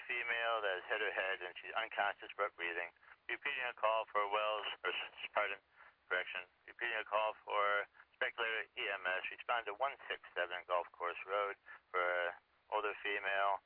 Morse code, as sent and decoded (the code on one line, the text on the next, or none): none